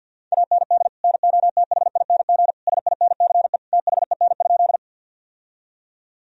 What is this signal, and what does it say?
Morse code, heard as a keyed tone, about 52 wpm, 700 Hz. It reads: AND NOTHING SINCE THEN?
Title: Sentences from Top 1000 Words - Review Part 1 - 50wpm